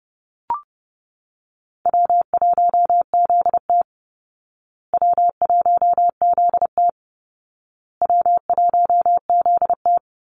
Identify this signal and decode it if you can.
Morse code, heard as a keyed tone, about 30 wpm, 700 Hz. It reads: W1ZT W1ZT W1ZT
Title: Call Signs Repeated 2x - 30wpm